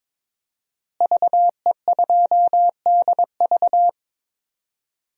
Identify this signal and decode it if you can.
Morse code, heard as a keyed tone, about 22 wpm, 700 Hz. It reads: VE2DV
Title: DXCC-100 Call Signs - 5 Characters - Repeated 2x - 22wpm